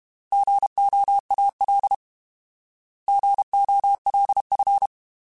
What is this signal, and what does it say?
Morse code, heard as a keyed tone, about 32 wpm, 800 Hz. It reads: GOAL GOLF
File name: Alphabetical / Alphabetical mp3s / Gs @30